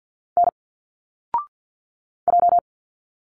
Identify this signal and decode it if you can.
Morse code, heard as a keyed tone, about 46 wpm, 700 Hz. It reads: N J
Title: Single Letters 50wpm